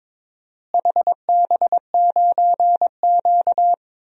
Morse code, heard as a keyed tone, about 22 wpm, 700 Hz. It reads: HB9Q
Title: DXCC-100 Call Signs - 4 Characters - Repeated 2x - 22wpm